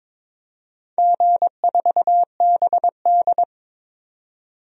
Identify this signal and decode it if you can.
Morse code, heard as a keyed tone, about 22 wpm, 700 Hz. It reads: G4BD
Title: DXCC-100 Call Signs - 4 Characters - 22wpm